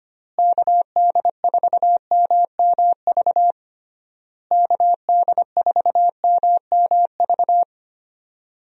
Morse code, heard as a keyed tone, about 25 wpm, 700 Hz. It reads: KD4MMV KD4MMV
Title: USA Call Signs - 6 Characters - Repeated 3x - 25wpm